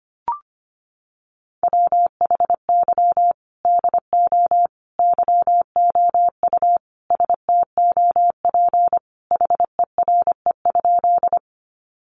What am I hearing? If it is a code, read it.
Morse code, heard as a keyed tone, about 25 wpm, 700 Hz. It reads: WHY DO YOU STOP HERE?
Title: Sentences from Top 300 Words 25wpm